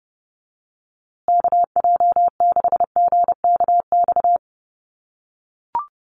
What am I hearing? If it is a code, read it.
Morse code, heard as a keyed tone, about 29 wpm, 700 Hz. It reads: KJ6GKX
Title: USA Call Signs Repeated 3x 30wpm